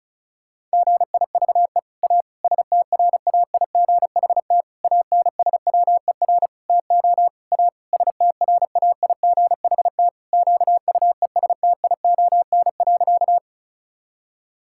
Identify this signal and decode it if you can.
Morse code, heard as a keyed tone, about 36 wpm, 700 Hz. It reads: GIVE A STRAIGHT ANSWER TO A STRAIGHT QUESTION.
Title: Sentences from Top 1000 Words - Review Part 6 - 35wpm